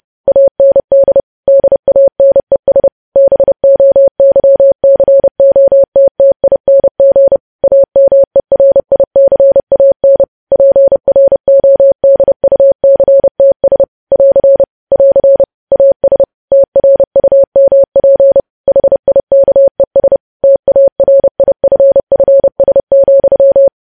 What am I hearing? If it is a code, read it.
Morse code, heard as a keyed tone, about 30 wpm, 550 Hz. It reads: AND DANES BOYCOTTING AMERICAN PRODUCTS <AR> <AR> AS TRUMP HIKES TARIFFS,
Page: news headlines in morse code at 30 wpm